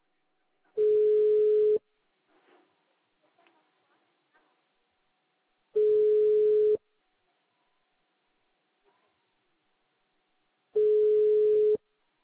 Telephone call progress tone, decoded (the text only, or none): ringback tone